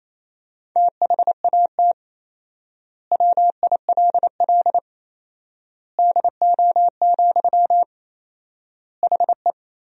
Morse code, heard as a keyed tone, about 28 wpm, 700 Hz. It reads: THAT WILL DO, HE
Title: 4x Word-Spacing - Sentences from Top 100 Words - 28wpm